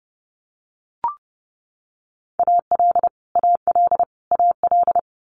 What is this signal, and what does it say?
Morse code, heard as a keyed tone, about 30 wpm, 700 Hz. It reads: AL AL AL
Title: Multi-letter Phonemes Repeated 3x 30wpm